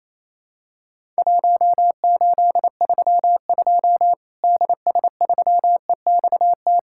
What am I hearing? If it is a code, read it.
Morse code, heard as a keyed tone, about 27 wpm, 700 Hz. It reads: 1832 DS3EXT
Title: Worked All Europe DX Contest - QTC Exchange - 28wpm